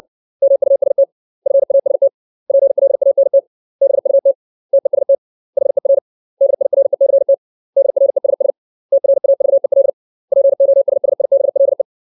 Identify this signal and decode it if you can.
Morse code, heard as a keyed tone, about 60 wpm, 550 Hz. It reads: MUST FAST PLANT BUT TEST HER DIRECT DRESS TRAVEL POSSIBLE
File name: 632 words / 632 Words 60WPM